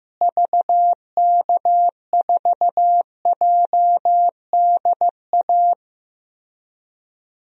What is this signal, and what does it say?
Morse code, heard as a keyed tone, about 15 wpm, 700 Hz. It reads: VK4JDA
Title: DXCC-100 Call Signs - 6 Characters - Repeated 3x - 15wpm